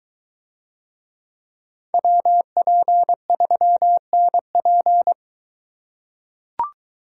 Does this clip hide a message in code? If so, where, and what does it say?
Yes: Morse code from 1.9 to 5.1 s, heard as a keyed tone, about 22 wpm, 700 Hz. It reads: WP3NP